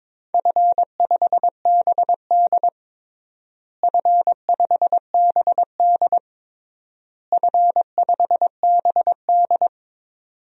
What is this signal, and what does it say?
Morse code, heard as a keyed tone, about 22 wpm, 700 Hz. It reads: F5BD F5BD F5BD
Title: DXCC-100 Call Signs - 4 Characters - Repeated 3x - 22wpm